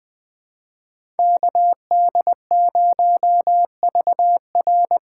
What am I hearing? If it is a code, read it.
Morse code, heard as a keyed tone, about 20 wpm, 700 Hz. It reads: KD0VR
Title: USA Call Signs - 6 Characters - 6x Character Spacing - 20wpm